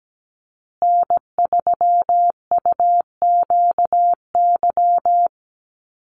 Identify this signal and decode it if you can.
Morse code, heard as a keyed tone, about 17 wpm, 700 Hz. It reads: N3UQY
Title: USA Call Signs Repeated 3x 17wpm